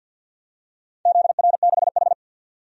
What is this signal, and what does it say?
Morse code, heard as a keyed tone, about 47 wpm, 700 Hz. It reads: ZR6H
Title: DXCC-100 Call Signs - 4 Characters - 50wpm